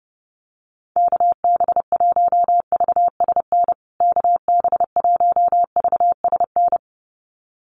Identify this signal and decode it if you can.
Morse code, heard as a keyed tone, about 30 wpm, 700 Hz. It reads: KB1VSN KB1VSN
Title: USA Call Signs Repeated 2x 30wpm